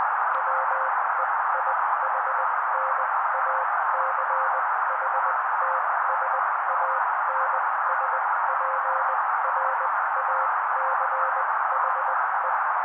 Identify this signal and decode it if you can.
Morse code, heard as a keyed tone, about 20 wpm, spaced as for 16 wpm, 530 Hz. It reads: WEIHNACHTSANSPRACHE
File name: daily / 16wpm noise